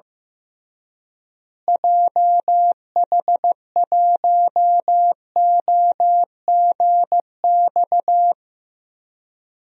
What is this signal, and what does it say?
Morse code, heard as a keyed tone, about 15 wpm, 700 Hz. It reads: JH1OGX
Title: DXCC-100 Call Signs - 6 Characters - Repeated 3x - 15wpm